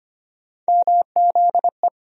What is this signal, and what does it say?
Morse code, heard as a keyed tone, about 25 wpm, 700 Hz. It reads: MZE
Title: USA Call Sign Postfixes - 3 Letters 25wpm